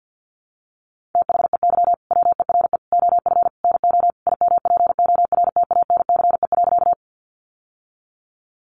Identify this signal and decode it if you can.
Morse code, heard as a keyed tone, about 50 wpm, 700 Hz. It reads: THEY WERE OF NO IMPORTANCE.
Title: Sentences from Top 1200 Words 50wpm